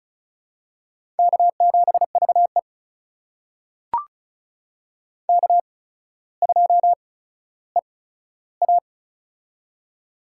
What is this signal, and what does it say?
Morse code, heard as a keyed tone, about 36 wpm, spaced as for 15 wpm, 700 Hz. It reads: K7VE K2EA
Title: USA Call Signs - 4 Characters - 8x Character Spacing - 35wpm